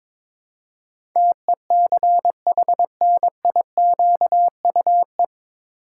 Morse code, heard as a keyed tone, about 22 wpm, 700 Hz. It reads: TECHNIQUE
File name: 9 Letter Words - First 3 Letters Given - 22wpm